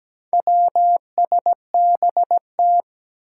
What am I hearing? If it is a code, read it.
Morse code, heard as a keyed tone, about 17 wpm, 700 Hz. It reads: WSBT
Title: Four Challenging Letter-Number Combinations 17wpm